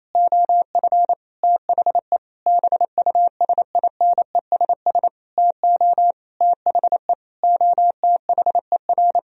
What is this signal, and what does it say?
Morse code, heard as a keyed tone, about 28 wpm, 700 Hz. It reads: OF THE BUSINESS TO THE OTHER
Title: Sentences from Top 1000 Words - Review Part 4 - 28wpm